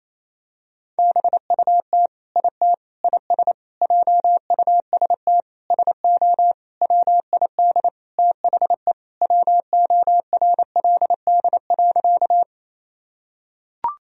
Morse code, heard as a keyed tone, about 28 wpm, 700 Hz. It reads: BUT IT IS JUST SO WID THE WORLD.
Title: Sentences from Top 200 Words and Abbreviations 28wpm